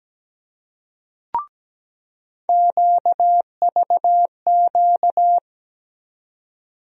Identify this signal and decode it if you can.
Morse code, heard as a keyed tone, about 17 wpm, 700 Hz. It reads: QVQ
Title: USA Call Sign Postfixes - 3 Letters 17wpm